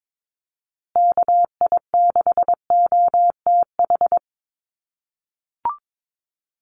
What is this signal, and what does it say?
Morse code, heard as a keyed tone, about 22 wpm, 700 Hz. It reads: KI6OTH
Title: USA Call Signs Repeated 2x 22wpm